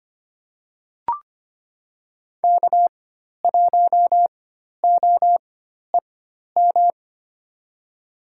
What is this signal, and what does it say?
Morse code, heard as a keyed tone, about 25 wpm, spaced as for 12 wpm, 700 Hz. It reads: K1OEM
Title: USA Call Signs - 5 Characters - 4x Character Spacing - 25wpm